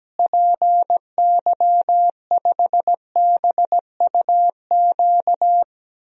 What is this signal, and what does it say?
Morse code, heard as a keyed tone, about 17 wpm, 700 Hz. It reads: PY5BUQ
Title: DXCC-100 Call Signs - 6 Characters - Repeated 3x - 17wpm